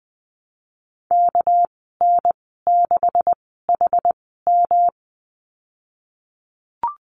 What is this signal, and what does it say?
Morse code, heard as a keyed tone, about 20 wpm, spaced as for 15 wpm, 700 Hz. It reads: KN6HM
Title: 213 Lesson 34g - Number 8 - Call Signs 20wpm